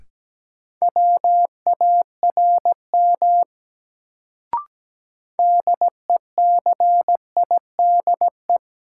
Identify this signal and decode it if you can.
Morse code, heard as a keyed tone, about 17 wpm, 700 Hz. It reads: WARM DECIDE
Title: Sentences from Top 500 Words - New Word Drill - 17wpm